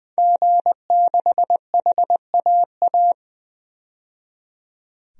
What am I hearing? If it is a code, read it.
Morse code, heard as a keyed tone, about 20 wpm, 700 Hz. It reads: G6HAA
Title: DXCC-100 Call Signs - 5 Characters - 20wpm